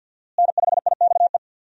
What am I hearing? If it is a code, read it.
Morse code, heard as a keyed tone, about 45 wpm, 700 Hz. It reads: N5IXE
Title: USA Call Signs - 5 Characters - Repeated 3x - 50wpm